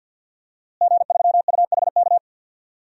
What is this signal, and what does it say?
Morse code, heard as a keyed tone, about 52 wpm, 700 Hz. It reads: G3UHK
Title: DXCC-100 Call Signs - 5 Characters - Repeated 3x - 50wpm